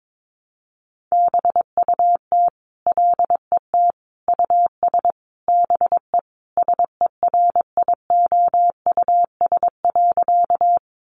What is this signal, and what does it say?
Morse code, heard as a keyed tone, about 22 wpm, 700 Hz. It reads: BUT LET US BE SERIOUS.
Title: Sentences from Top 1100 Words 22wpm